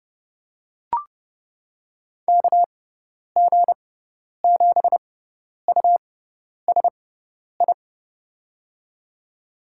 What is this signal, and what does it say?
Morse code, heard as a keyed tone, about 31 wpm, spaced as for 10 wpm, 700 Hz. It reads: KG7USI
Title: USA Call Signs - 6x Character Spacing - 30wpm